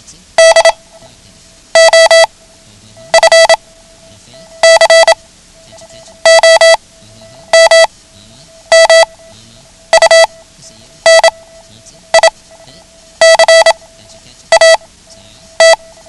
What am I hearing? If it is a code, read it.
Morse code, heard as a keyed tone, about 27 wpm, spaced as for 9 wpm, 660 Hz. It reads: D OF COMMUNICAT
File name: ham radio / CodeQuick / Code Quick 09